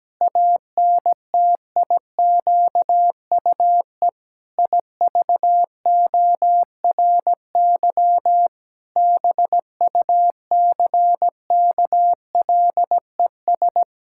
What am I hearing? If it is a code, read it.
Morse code, heard as a keyed tone, about 17 wpm, 700 Hz. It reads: ANTIQUE IVORY BUCKLES
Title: Pangram Challenge v2 - 17wpm